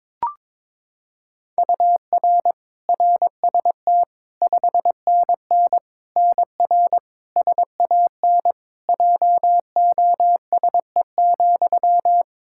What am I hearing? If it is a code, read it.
Morse code, heard as a keyed tone, about 22 wpm, 700 Hz. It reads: UR RST 5NN NR SAN JOSE,
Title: Practice QSO - Exchange 1 - Beginning - 22wpm